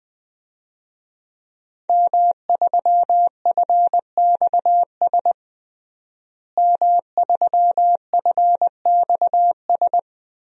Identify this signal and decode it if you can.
Morse code, heard as a keyed tone, about 20 wpm, 700 Hz. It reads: M3FXS M3FXS
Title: DXCC-100 Call Signs - 5 Characters - Repeated 2x - 20wpm